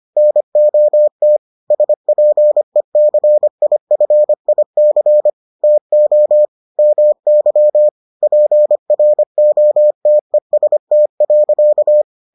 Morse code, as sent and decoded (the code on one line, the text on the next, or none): -. --- - / ... .--. -.-. .. ..-. .. -.-. / - --- / -- -.-- / .--. .-. --- - . ... - .-.-.-
NOT SPECIFIC TO MY PROTEST.